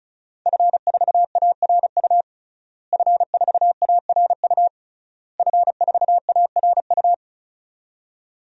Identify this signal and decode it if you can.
Morse code, heard as a keyed tone, about 36 wpm, 700 Hz. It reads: F4ARU F4ARU F4ARU
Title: DXCC-100 Call Signs - 5 Characters - Repeated 3x - 35wpm